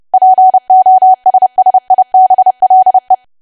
Morse code, heard as a keyed tone, about 30 wpm, 750 Hz. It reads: POSSIBLE
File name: cwa / W1AW-Level3 / 30WPM